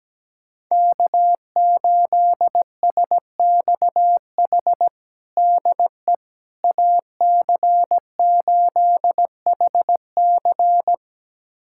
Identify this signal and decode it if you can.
Morse code, heard as a keyed tone, about 17 wpm, 700 Hz. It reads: K8SXH DE AC8HC
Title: USA Callsign DE Callsign 17wpm